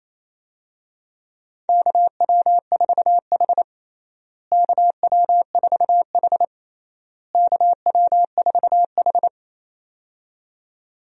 Morse code, heard as a keyed tone, about 28 wpm, 700 Hz. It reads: KW4H KW4H KW4H
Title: USA Call Signs - 4 Characters - Repeated 3x - 28wpm